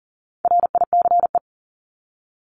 Morse code, heard as a keyed tone, about 40 wpm, 700 Hz. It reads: RICE
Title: Top 1501-1750 Words 40wpm